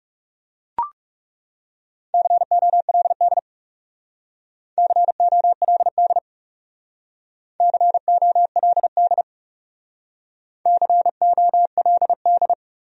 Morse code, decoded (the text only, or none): COLD COLD COLD COLD